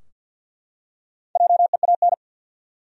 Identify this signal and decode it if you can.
Morse code, heard as a keyed tone, about 52 wpm, 700 Hz. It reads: JEAN
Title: USA Common Names - Max 4 Letters - 50wpm